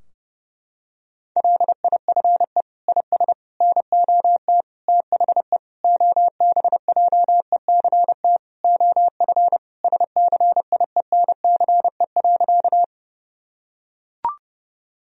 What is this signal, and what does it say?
Morse code, heard as a keyed tone, about 30 wpm, 700 Hz. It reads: LIFE IS NOT THE OBJECT OF SCIENCE.
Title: Sentences from Top 1000 Words - Review Part 3 - 30wpm